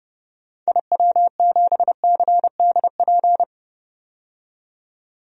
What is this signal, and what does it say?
Morse code, heard as a keyed tone, about 30 wpm, 700 Hz. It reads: IW7CDP
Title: DXCC-100 Call Signs - 6 Characters - 30wpm